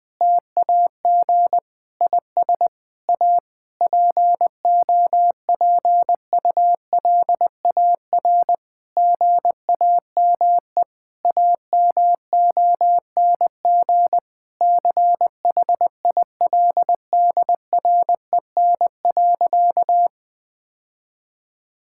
Morse code, heard as a keyed tone, about 20 wpm, 700 Hz. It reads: TAG IS A POPULAR GAME AMONG CHILDREN.
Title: Difficult Sentences with Hints - 20wpm